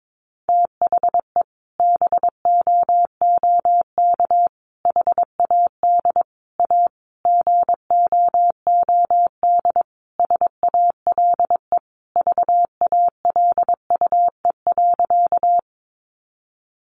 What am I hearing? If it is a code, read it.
Morse code, heard as a keyed tone, about 22 wpm, 700 Hz. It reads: THE BOOK HAD A GOOD SALE VALUE.
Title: Sentences from Top 1400 Words 22wpm